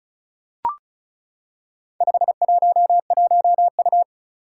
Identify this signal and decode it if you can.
Morse code, heard as a keyed tone, about 36 wpm, 700 Hz. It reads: 511U
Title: Four Challenging Letter-Number Combinations 35wpm